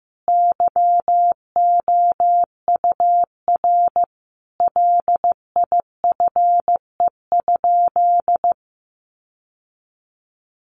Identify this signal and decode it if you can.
Morse code, heard as a keyed tone, about 15 wpm, 700 Hz. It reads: YOUR LIFE?
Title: Sentences from Top 900 Words 15wpm